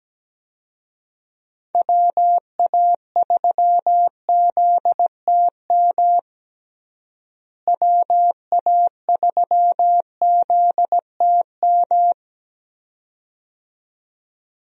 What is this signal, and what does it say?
Morse code, heard as a keyed tone, about 17 wpm, 700 Hz. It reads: WA3ZTM WA3ZTM
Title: USA Call Signs - 6 Characters - Repeated 2x - 17wpm